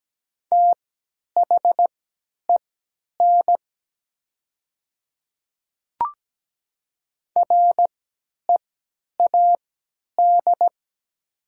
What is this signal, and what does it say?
Morse code, heard as a keyed tone, about 17 wpm, spaced as for 10 wpm, 700 Hz. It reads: THEN READ
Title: 3x Character Spacing - Top 200 Words - 17wpm